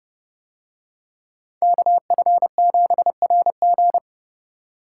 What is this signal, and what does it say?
Morse code, heard as a keyed tone, about 29 wpm, 700 Hz. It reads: KF7RG